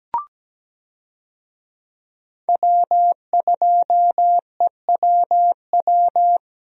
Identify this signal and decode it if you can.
Morse code, heard as a keyed tone, about 17 wpm, 700 Hz. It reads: W2EWW
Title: USA Call Signs - 5 Characters - Repeated 2x - 17wpm